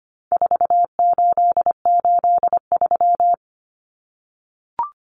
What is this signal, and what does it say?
Morse code, heard as a keyed tone, about 25 wpm, 700 Hz. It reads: 4883
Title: Numbers - 4-Digit 25wpm